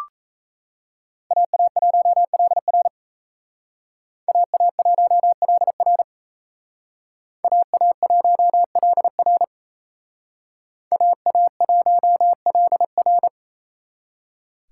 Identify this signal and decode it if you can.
Morse code, heard as a keyed tone, about 32 wpm, 700 Hz. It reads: AA1LR AA1LR AA1LR AA1LR
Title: Speed Racing - USA 2x2 Call Signs 28wpm